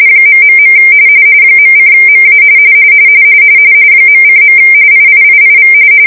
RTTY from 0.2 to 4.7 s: ... .:MA .:.